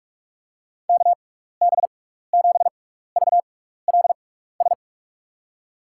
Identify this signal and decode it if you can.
Morse code, heard as a keyed tone, about 46 wpm, spaced as for 16 wpm, 700 Hz. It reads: KB7VLS
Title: USA Call Signs - 6 Characters - 6x Character Spacing - 45wpm